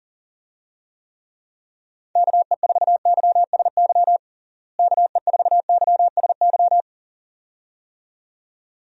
Morse code, heard as a keyed tone, about 40 wpm, 700 Hz. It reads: KE4YSY KE4YSY
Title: USA Call Signs - 6 Characters - Repeated 2x - 40wpm